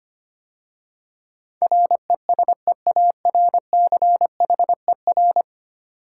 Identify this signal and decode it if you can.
Morse code, heard as a keyed tone, about 25 wpm, 700 Hz. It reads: RESEARCHER
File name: 10 Letter Words - First 3 Letters Given - 25wpm